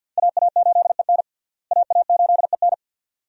Morse code, heard as a keyed tone, about 46 wpm, 700 Hz. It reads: AA8EN AA8EN
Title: USA Call Signs - 5 Characters - Repeated 2x - 50wpm